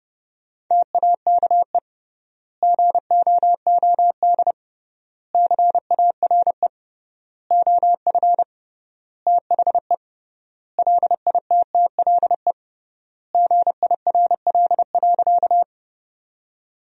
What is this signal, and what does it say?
Morse code, heard as a keyed tone, about 30 wpm, 700 Hz. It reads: TAKE GOOD CARE OF THE LITTLE GIRL.
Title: 3x Word-Spacing - Sentences from Top 400 Words - 30wpm